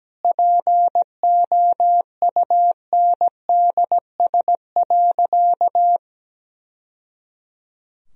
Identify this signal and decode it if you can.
Morse code, heard as a keyed tone, about 17 wpm, 700 Hz. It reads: POUNDS.
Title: Sentences from Top 1000 Words - Review Part 4 - 17wpm